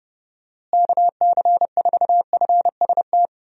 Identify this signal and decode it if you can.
Morse code, heard as a keyed tone, about 31 wpm, 700 Hz. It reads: KC4FST
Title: USA Call Signs - 6 Characters - Repeated 2x - 30wpm